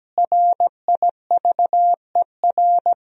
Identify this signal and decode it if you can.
Morse code, heard as a keyed tone, about 17 wpm, 700 Hz. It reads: RIVER
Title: 4x Word-Spacing - Sentences from Top 300 Words - 17wpm